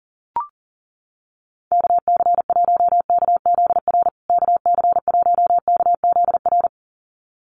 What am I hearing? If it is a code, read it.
Morse code, heard as a keyed tone, about 40 wpm, 700 Hz. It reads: KC1KZR KC1KZR
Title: USA Call Signs Repeated 2x 40wpm